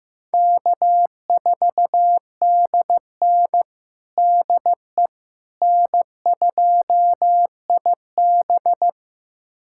Morse code, heard as a keyed tone, about 15 wpm, 700 Hz. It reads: K4DN DE N2IB
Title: USA Callsign DE Callsign - 4 Characters - 15wpm